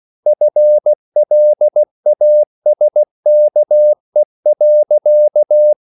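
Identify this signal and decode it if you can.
Morse code, heard as a keyed tone, about 16 wpm, 600 Hz. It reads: FLASKE.
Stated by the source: Slow (QRS) Morse code at 16 wpm, from CW med Gnister.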